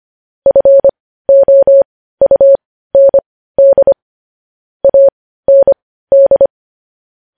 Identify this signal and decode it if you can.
Morse code, heard as a keyed tone, about 25 wpm, spaced as for 15 wpm, 550 Hz. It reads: FOUND AND
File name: podcast / News headlines-1x-25x15